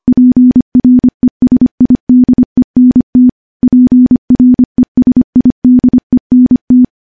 Morse code, heard as a keyed tone, about 25 wpm, 250 Hz. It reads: PRESIDENT PRESIDENT
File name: morse-training-25WPM